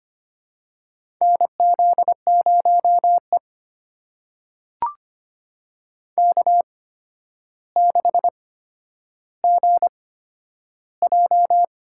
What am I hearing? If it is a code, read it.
Morse code, heard as a keyed tone, about 25 wpm, spaced as for 11 wpm, 700 Hz. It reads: NZ0E K6GJ
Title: USA Call Signs - 4 Characters - 8x Character Spacing - 25wpm